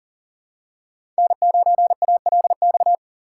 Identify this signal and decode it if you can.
Morse code, heard as a keyed tone, about 40 wpm, 700 Hz. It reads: N9ALX
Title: USA Call Signs - 5 Characters - 2x Character Spacing - 40wpm